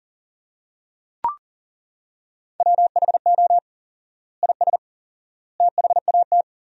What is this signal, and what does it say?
Morse code, heard as a keyed tone, about 41 wpm, 700 Hz. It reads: WHO IS THAT
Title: 4x Word-Spacing - Sentences from Top 200 Words - 40wpm